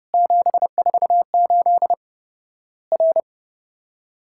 Morse code, heard as a keyed tone, about 31 wpm, 700 Hz. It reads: 748 R
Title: Worked All Europe DX Contest - QTC Exchange - 30wpm